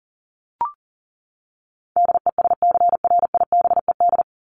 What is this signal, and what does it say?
Morse code, heard as a keyed tone, about 40 wpm, 700 Hz. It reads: DESCRIBED
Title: ED Suffix - Top 100 Words with Base Words 40wpm